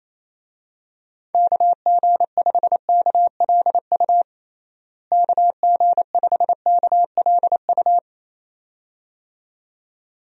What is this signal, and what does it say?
Morse code, heard as a keyed tone, about 27 wpm, 700 Hz. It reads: KG5KLU KG5KLU
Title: USA Call Signs - 6 Characters - Repeated 2x - 28wpm